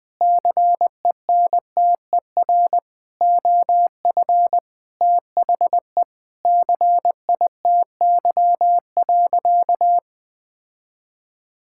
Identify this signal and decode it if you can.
Morse code, heard as a keyed tone, about 20 wpm, 700 Hz. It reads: CENTER OF THE CITY.
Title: Sentences from Top 1000 Words - Review Part 6 - 20wpm